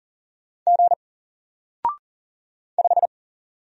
Morse code, heard as a keyed tone, about 40 wpm, 700 Hz. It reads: G 5
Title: Single Letter-Number 40wpm